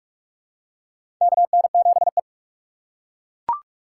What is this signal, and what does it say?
Morse code, heard as a keyed tone, about 41 wpm, 700 Hz. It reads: KN7E
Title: USA Call Signs - 8x Character Spacing - 45wpm